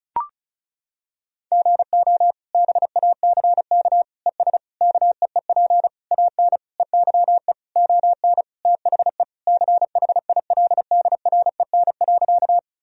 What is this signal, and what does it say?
Morse code, heard as a keyed tone, about 36 wpm, 700 Hz. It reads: GO BACK ES KEEP AN EYE ON THE CHILDREN.
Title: Sentences from Top 300 Words and Abbreviations 35wpm